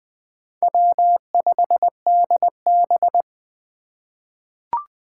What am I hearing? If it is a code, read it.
Morse code, heard as a keyed tone, about 20 wpm, 700 Hz. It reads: W5DB
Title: USA Call Signs - 4 Characters - 4x Character Spacing - 20wpm